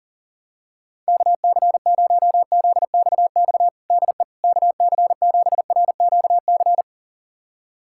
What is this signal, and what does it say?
Morse code, heard as a keyed tone, about 37 wpm, 700 Hz. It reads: KC0ZXX DE KC7RQC
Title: USA Callsign DE Callsign 40wpm